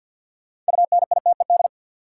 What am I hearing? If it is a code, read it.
Morse code, heard as a keyed tone, about 47 wpm, 700 Hz. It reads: UNITED
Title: Sets of 2 Words - New Word Drill - 50wpm